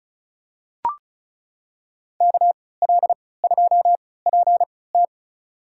Morse code, heard as a keyed tone, about 36 wpm, spaced as for 20 wpm, 700 Hz. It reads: KL2PT
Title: USA Call Signs - 5 Characters - 3x Character Spacing - 35wpm